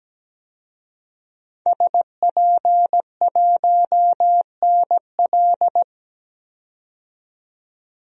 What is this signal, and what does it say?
Morse code, heard as a keyed tone, about 17 wpm, 700 Hz. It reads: SP1NL
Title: DXCC-100 Call Signs - 5 Characters - 17wpm